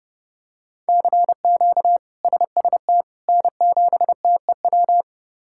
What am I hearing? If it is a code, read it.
Morse code, heard as a keyed tone, about 30 wpm, 700 Hz. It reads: CQ SST N7TEW
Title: K1USN SST QSO Practice - 30wpm